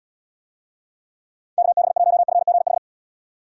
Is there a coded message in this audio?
Yes: Morse code from 1.6 to 2.8 s, heard as a keyed tone, about 74 wpm, 700 Hz. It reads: KB1FGH